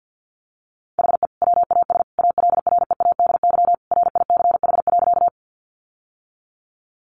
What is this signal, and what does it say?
Morse code, heard as a keyed tone, about 50 wpm, 700 Hz. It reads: HE WAS ALREADY RICH.
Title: Sentences from Top 1100 Words 50wpm